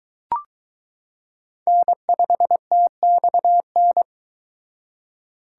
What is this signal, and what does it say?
Morse code, heard as a keyed tone, about 23 wpm, 700 Hz. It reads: N5TXN